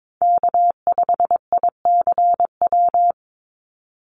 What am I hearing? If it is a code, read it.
Morse code, heard as a keyed tone, about 22 wpm, 700 Hz. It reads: K5ICW
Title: USA Call Signs 22wpm